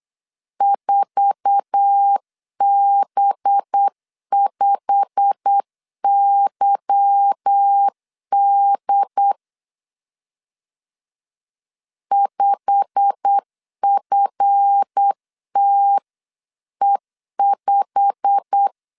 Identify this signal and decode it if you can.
Morse code, heard as a keyed tone, about 8 wpm, 800 Hz. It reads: 4B5YD 5FTE5
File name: CW / DET / 1 9 1500C S C 8wpm 5653